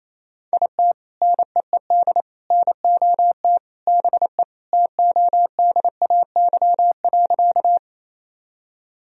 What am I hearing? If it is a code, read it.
Morse code, heard as a keyed tone, about 27 wpm, 700 Hz. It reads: IT NEED NOT BE TODAY.
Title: Sentences from Top 1000 Words - Review Part 4 - 28wpm